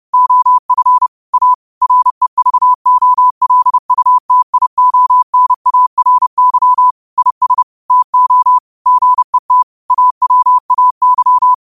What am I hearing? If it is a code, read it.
Morse code, heard as a keyed tone, about 30 wpm, 1000 Hz. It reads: OF A REVOLUTIONARY IS TO GET AWAY
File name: qotdfiles / 30wpm